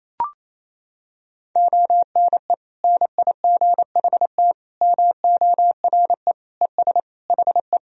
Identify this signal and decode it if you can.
Morse code, heard as a keyed tone, about 28 wpm, 700 Hz. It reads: ONE NIGHT MORE ES HE
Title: Sentences from Top 300 Words and Abbreviations 28wpm